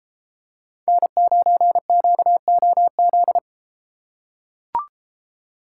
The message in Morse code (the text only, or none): N9QOZ